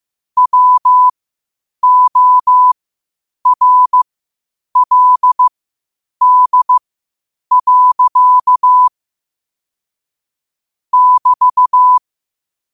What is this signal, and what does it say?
Morse code, heard as a keyed tone, about 15 wpm, spaced as for 8 wpm, 1000 Hz. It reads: WORLD. <BT>
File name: qotdfiles / 8wpm